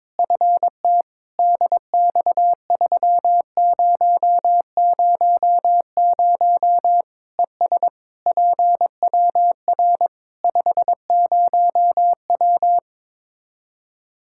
Morse code, heard as a keyed tone, about 22 wpm, 700 Hz. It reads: FT DX3000 ES PWR 50W
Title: Practice QSO - Exchange 2 - Info - 22wpm